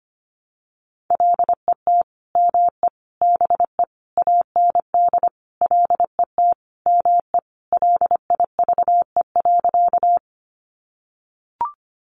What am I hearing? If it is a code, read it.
Morse code, heard as a keyed tone, about 25 wpm, 700 Hz. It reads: LET ME BE AND LET ME LIVE.
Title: Sentences from Top 300 Words 25wpm